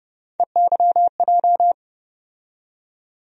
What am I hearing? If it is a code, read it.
Morse code, heard as a keyed tone, about 30 wpm, 700 Hz. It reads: EYJ
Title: USA Call Sign Postfixes - 3 Letters 30wpm